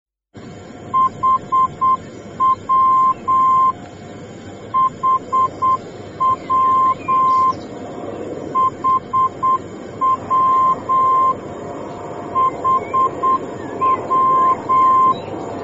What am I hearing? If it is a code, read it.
Morse code, heard as a keyed tone, about 8 wpm, 1050 Hz, with noise. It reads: HW HW HW HW